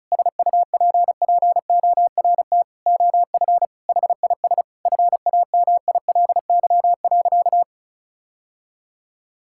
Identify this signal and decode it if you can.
Morse code, heard as a keyed tone, about 36 wpm, 700 Hz. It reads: SUPPORT OF HIS FAMILY.
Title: Sentences from Top 1000 Words - Review Part 5 - 35wpm